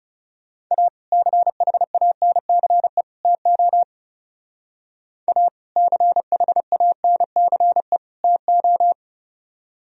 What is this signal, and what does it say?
Morse code, heard as a keyed tone, about 36 wpm, 700 Hz. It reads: A CHANCE TO A CHANCE TO
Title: Speed-Racing - Sets of 3 Words 30wpm